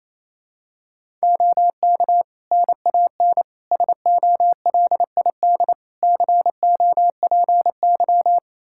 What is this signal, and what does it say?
Morse code, heard as a keyed tone, about 28 wpm, 700 Hz. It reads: OK NAN SOLID COPY
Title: Practice QSO - Exchange 3 - More Info - 28wpm